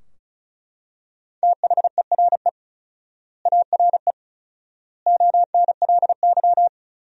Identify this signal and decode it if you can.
Morse code, heard as a keyed tone, about 34 wpm, 700 Hz. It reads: THERE ARE ONLY
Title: 4x Word-Spacing - Sentences from Top 300 Words - 35wpm